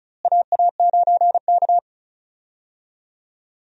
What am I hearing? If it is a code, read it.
Morse code, heard as a keyed tone, about 34 wpm, 700 Hz. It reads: AA9K
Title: USA Call Signs - 4 Characters - Repeated 2x - 35wpm